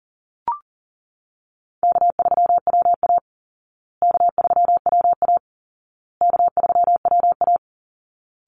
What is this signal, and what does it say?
Morse code, heard as a keyed tone, about 40 wpm, 700 Hz. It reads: K3WA K3WA K3WA
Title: Call Signs Repeated 2x - 40wpm